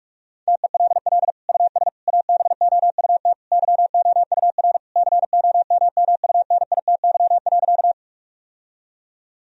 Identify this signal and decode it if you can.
Morse code, heard as a keyed tone, about 46 wpm, 700 Hz. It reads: TELL US ABOUT YOUR COMMUNITY.
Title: Sentences from Top 1000 Words - Review Part 4 - 45wpm